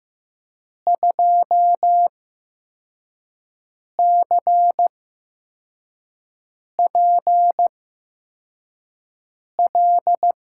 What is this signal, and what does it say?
Morse code, heard as a keyed tone, about 15 wpm, spaced as for 4 wpm, 700 Hz. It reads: 2CPL
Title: USA Call Signs - 6 Characters - 8x Character Spacing - 15wpm